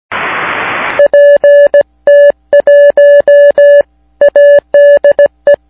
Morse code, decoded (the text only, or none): PT1 ADE